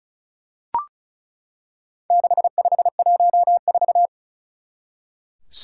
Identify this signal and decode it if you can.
Morse code, heard as a keyed tone, about 36 wpm, 700 Hz. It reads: 6514
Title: Four Challenging Letter-Number Combinations 35wpm